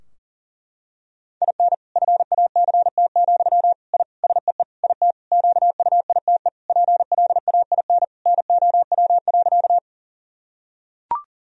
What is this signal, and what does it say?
Morse code, heard as a keyed tone, about 37 wpm, 700 Hz. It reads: IN FACT, I SEE IT QUITE PLAIN NOW.
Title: Sentences from Top 1000 Words - Review Part 3 - 40wpm